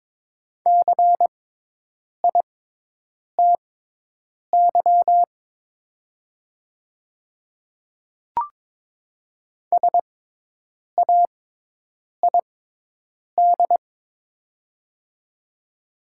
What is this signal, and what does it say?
Morse code, heard as a keyed tone, about 22 wpm, spaced as for 8 wpm, 700 Hz. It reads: CITY SAID
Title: 6x Character Spacing - Top 300 Words - 22wpm